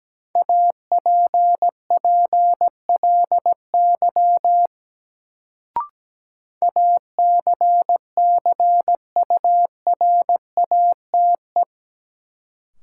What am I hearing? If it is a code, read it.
Morse code, heard as a keyed tone, about 17 wpm, 700 Hz. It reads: APPLY ACCURATE
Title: Sentences from Top 1200 Words - New Word Drill - 17wpm